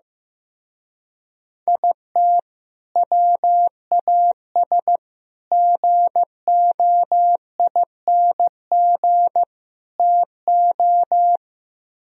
Morse code, heard as a keyed tone, about 15 wpm, 700 Hz. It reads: IT WAS GOING TO
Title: Speed-Racing - Sets of 4 Words 15wpm